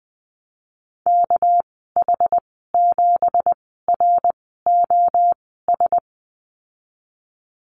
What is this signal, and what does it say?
Morse code, heard as a keyed tone, about 20 wpm, spaced as for 15 wpm, 700 Hz. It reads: KH7ROS
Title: 207 Lesson 33g - Letter j - Call Signs 20wpm